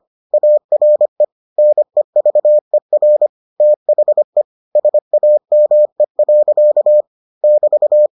Morse code, heard as a keyed tone, about 25 wpm, 600 Hz. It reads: ARE NEVER THE SAME. <BT>